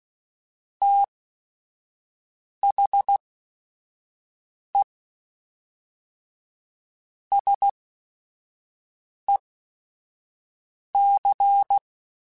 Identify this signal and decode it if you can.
Morse code, heard as a keyed tone, about 16 wpm, spaced as for 5 wpm, 800 Hz. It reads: THE SEC
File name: Technical Subjects / Technical Subjects-mp3s / Sw Pwr Sup @5